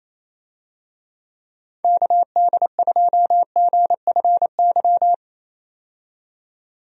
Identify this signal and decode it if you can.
Morse code, heard as a keyed tone, about 28 wpm, 700 Hz. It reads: KD2GFY
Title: USA Call Signs - 6 Characters - 28wpm